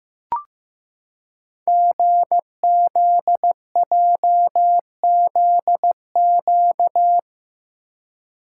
Morse code, heard as a keyed tone, about 15 wpm, 700 Hz. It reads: GZJZQ
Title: Code Groups - 5 Letter:Numbers - 15wpm